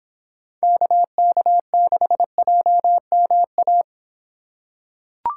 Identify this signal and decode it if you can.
Morse code, heard as a keyed tone, about 24 wpm, 700 Hz. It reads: KK6JMA